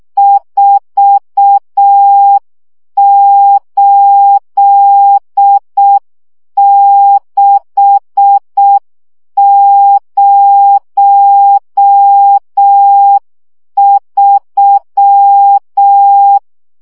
Morse code, heard as a keyed tone, about 6 wpm, 800 Hz. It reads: 48603
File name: morze vizsga 2011-1 5